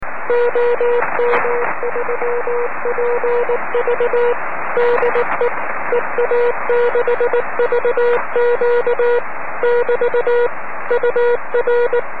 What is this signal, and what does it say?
Morse code, heard as a keyed tone, about 19 wpm, 480 Hz, with radio noise. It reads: OM3PV DE EA6VQ <BT> UR
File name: om3pv 70mhz / zaznam / ea6vq